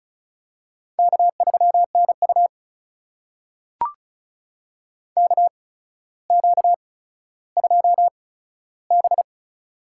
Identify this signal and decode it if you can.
Morse code, heard as a keyed tone, about 36 wpm, spaced as for 15 wpm, 700 Hz. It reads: K3NU KQ2B
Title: USA Call Signs - 4 Characters - 8x Character Spacing - 35wpm